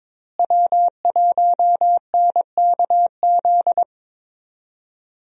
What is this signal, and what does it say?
Morse code, heard as a keyed tone, about 22 wpm, 700 Hz. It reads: W1NKZ
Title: USA Call Signs - 5 Characters - Repeated 3x - 22wpm